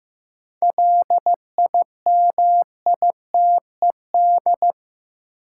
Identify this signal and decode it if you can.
Morse code, heard as a keyed tone, about 15 wpm, 700 Hz. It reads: LIMITED
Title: Sentences from Top 900 Words - New Word Drill - 15wpm